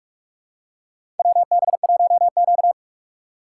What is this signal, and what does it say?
Morse code, heard as a keyed tone, about 44 wpm, 690 Hz. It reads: WB1Q
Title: USA Call Signs - 4 Characters - 4x Character Spacing - 45wpm